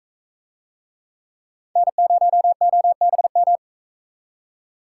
Morse code, heard as a keyed tone, about 44 wpm, 700 Hz. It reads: N0OBM